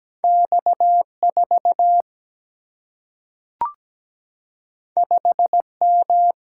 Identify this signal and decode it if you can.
Morse code, heard as a keyed tone, about 17 wpm, 700 Hz. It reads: X 4 5 M
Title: Single-Letter Single-Number 17wpm